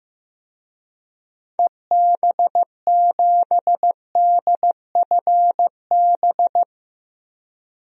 Morse code, heard as a keyed tone, about 15 wpm, 700 Hz. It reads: EB7DFB
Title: DXCC-100 Call Signs - 6 Characters - 15wpm